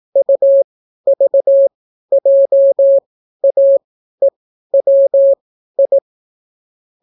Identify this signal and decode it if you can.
Morse code, heard as a keyed tone, about 18 wpm, spaced as for 12 wpm, 550 Hz. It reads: UVJAEWI